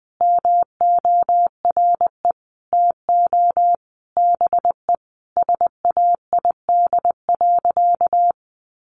Morse code, heard as a keyed tone, about 20 wpm, 700 Hz. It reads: MORE TO BE SAID.
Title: Sentences from Top 1000 Words 20wpm